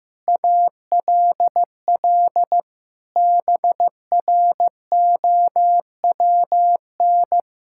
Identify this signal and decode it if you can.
Morse code, heard as a keyed tone, about 16 wpm, 700 Hz. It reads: ALL BROWN
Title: Pangram Challenge v2 - 15wpm